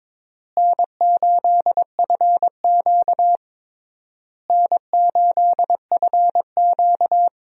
Morse code, heard as a keyed tone, about 22 wpm, 700 Hz. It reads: N8FQ N8FQ
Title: USA Call Signs - 4 Characters - Repeated 3x - 22wpm